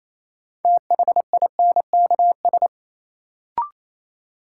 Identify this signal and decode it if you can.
Morse code, heard as a keyed tone, about 28 wpm, 700 Hz. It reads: THINKS
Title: Sentences from Top 1000 Words - Review Part 6 - New Word Drill - 28wpm